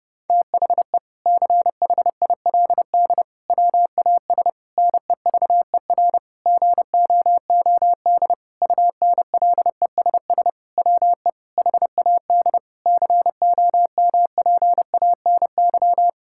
Morse code, heard as a keyed tone, about 30 wpm, 700 Hz. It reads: THE CHILD WAS NEVER GOOD UNLESS WE HAD COMPANY
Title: Sentences from Top 1000 Words - Review Part 6 - 30wpm